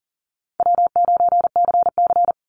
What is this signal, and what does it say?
Morse code, heard as a keyed tone, about 40 wpm, 700 Hz. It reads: W9CC
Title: Easy Call Signs Repeated 3x - 40wpm